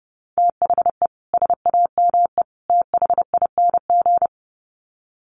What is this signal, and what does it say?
Morse code, heard as a keyed tone, about 30 wpm, 700 Hz. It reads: THE SAME THING
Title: Sets of 3 Words 30wpm